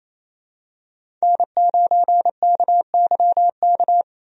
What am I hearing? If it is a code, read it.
Morse code, heard as a keyed tone, about 28 wpm, 700 Hz. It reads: N9KYK
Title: USA Call Signs - 5 Characters - Repeated 2x - 28wpm